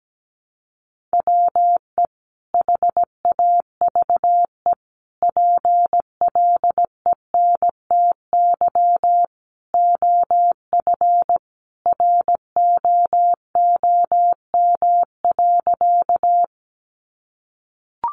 Morse code, heard as a keyed tone, about 17 wpm, 700 Hz. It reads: WE HAVE PLENTY OF ROOM.
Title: Sentences from Top 1300 Words 17wpm